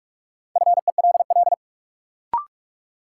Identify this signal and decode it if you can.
Morse code, heard as a keyed tone, about 41 wpm, 700 Hz. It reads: FELL
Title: Sets of 5 Words Encore - New Word Drill - 45wpm